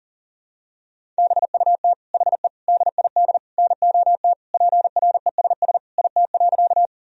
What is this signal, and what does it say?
Morse code, heard as a keyed tone, about 40 wpm, 700 Hz. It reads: BUT HE DID NOT PRESS IT.
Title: Sentences from Top 1000 Words - Review Part 1 - 40wpm